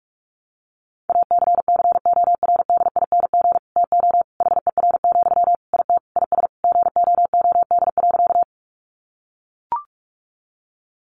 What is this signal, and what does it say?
Morse code, heard as a keyed tone, about 44 wpm, 700 Hz. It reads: ACCORDING TO HER, IT IS GOOD.
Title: Sentences from Top 800 Words 45wpm